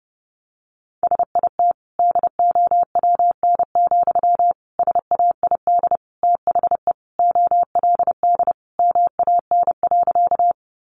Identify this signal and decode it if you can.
Morse code, heard as a keyed tone, about 30 wpm, 700 Hz. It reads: SIT DOWN, SAID THE OLD MAN.
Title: Sentences from Top 500 Words 30wpm